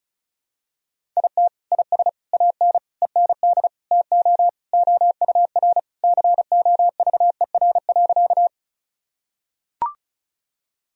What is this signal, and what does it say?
Morse code, heard as a keyed tone, about 34 wpm, 700 Hz. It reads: IT IS AN END TO OUR COVER.
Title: Sentences from Top 1000 Words - Review Part 1 - 35wpm